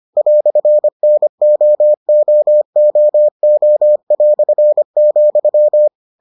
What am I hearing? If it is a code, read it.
Morse code, heard as a keyed tone, about 25 wpm, 600 Hz. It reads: "NOOOO",